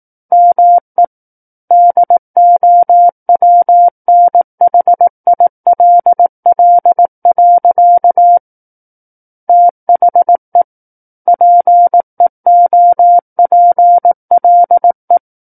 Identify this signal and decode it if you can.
Morse code, heard as a keyed tone, about 18 wpm, 700 Hz. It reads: ME DOWNHILL. THE PEOPLE
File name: 18 WPM